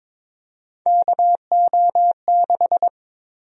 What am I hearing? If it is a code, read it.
Morse code, heard as a keyed tone, about 23 wpm, 700 Hz. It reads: KO6
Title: USA Call Sign Prefixes - Repeated 2x - 22wpm